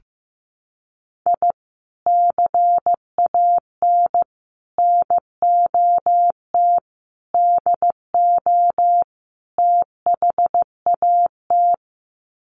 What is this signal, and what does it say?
Morse code, heard as a keyed tone, about 15 wpm, 700 Hz. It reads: I CAN NOT DO THAT
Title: Sets of 5 Words Encore2 15wpm